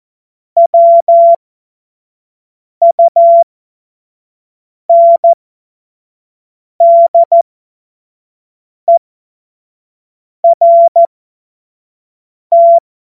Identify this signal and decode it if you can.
Morse code, heard as a keyed tone, about 14 wpm, spaced as for 5 wpm, 680 Hz. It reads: WUNDERT
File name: Morsekurs / Witze / Witz 21. Februar 2023 25 F70